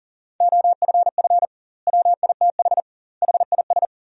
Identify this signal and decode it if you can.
Morse code, heard as a keyed tone, about 40 wpm, 700 Hz. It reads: OFF WITH HIS
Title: 2x Word-Spacing - Sentences from Top 300 Words - 40wpm